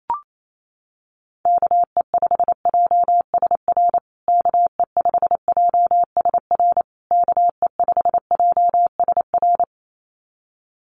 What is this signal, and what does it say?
Morse code, heard as a keyed tone, about 28 wpm, 700 Hz. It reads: KE5JSR KE5JSR KE5JSR
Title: USA Call Signs Repeated 3x 28wpm